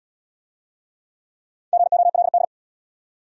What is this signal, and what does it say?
Morse code, heard as a keyed tone, about 65 wpm, 700 Hz. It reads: GOLD